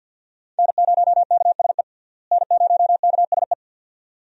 Morse code, heard as a keyed tone, about 50 wpm, 700 Hz. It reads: N0KSE N0KSE
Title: USA Call Signs - 5 Characters - Repeated 2x - 50wpm